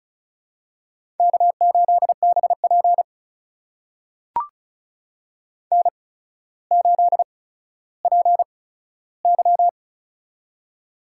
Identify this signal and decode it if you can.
Morse code, heard as a keyed tone, about 35 wpm, spaced as for 15 wpm, 700 Hz. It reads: K8BP N8PY
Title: USA Call Signs - 4 Characters - 8x Character Spacing - 35wpm